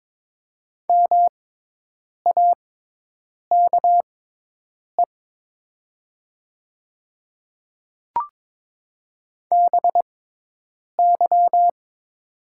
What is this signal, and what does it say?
Morse code, heard as a keyed tone, about 22 wpm, spaced as for 8 wpm, 700 Hz. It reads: MAKE BY
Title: 6x Character Spacing - Top 200 Words - 22wpm